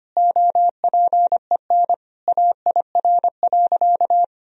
Morse code, heard as a keyed tone, about 25 wpm, 700 Hz. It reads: OPEN AIR.
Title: Sentences from Top 1000 Words - Review Part 1 - 25wpm